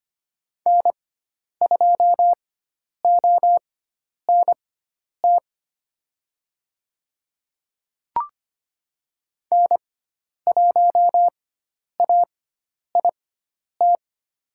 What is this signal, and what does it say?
Morse code, heard as a keyed tone, about 25 wpm, spaced as for 10 wpm, 700 Hz. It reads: N2ONT N1AIT
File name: Lesson 16g - Letter c - Call Signs 25wpm